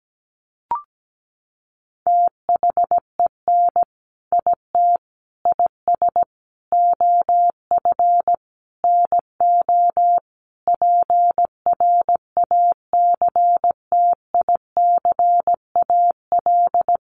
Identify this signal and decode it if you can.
Morse code, heard as a keyed tone, about 17 wpm, 700 Hz. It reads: THEN IT IS OF NO PRACTICAL
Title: Sentences from Top 1400 Words 17wpm